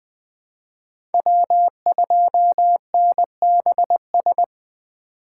Morse code, heard as a keyed tone, about 20 wpm, 700 Hz. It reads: W2NBS
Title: USA Call Signs - 5 Characters - 3x Character Spacing - 20wpm